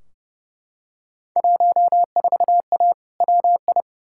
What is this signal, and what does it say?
Morse code, heard as a keyed tone, about 29 wpm, 700 Hz. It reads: 14A WI
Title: Field Day Exchanges Encore - 30wpm